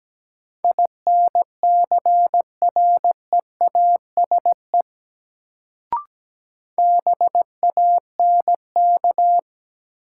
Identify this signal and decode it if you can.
Morse code, heard as a keyed tone, about 17 wpm, 700 Hz. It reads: INCREASE BANK
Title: Sets of 5 Words - New Word Drill - 17wpm